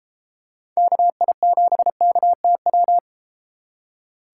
Morse code, heard as a keyed tone, about 33 wpm, 700 Hz. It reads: KI7KTW